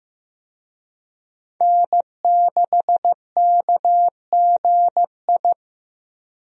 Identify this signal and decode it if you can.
Morse code, heard as a keyed tone, about 15 wpm, 700 Hz. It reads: N6KGI
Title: USA Call Signs - 5 Characters - Repeated 2x - 15wpm